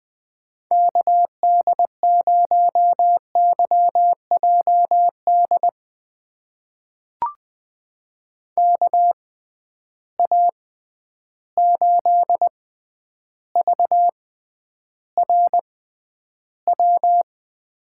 Morse code, heard as a keyed tone, about 20 wpm, spaced as for 10 wpm, 700 Hz. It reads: KD0YJD KA8VRW
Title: USA Call Signs - 6x Character Spacing - 20wpm